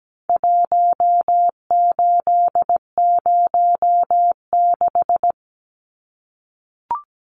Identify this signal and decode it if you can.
Morse code, heard as a keyed tone, about 17 wpm, 700 Hz. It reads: 1806
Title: Numbers - 4-Digit 17wpm